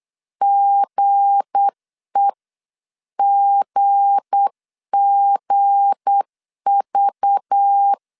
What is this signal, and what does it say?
Morse code, heard as a keyed tone, about 8 wpm, 800 Hz. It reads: GEGGV
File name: LEZIONI / 7 GF9Z / 7 G 8wpm 3118